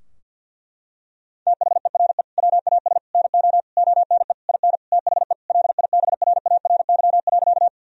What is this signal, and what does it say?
Morse code, heard as a keyed tone, about 50 wpm, 700 Hz. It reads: THERE WAS NO ONE IN THE LIBRARY.
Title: Sentences from Top 1000 Words - Review Part 4 - 50wpm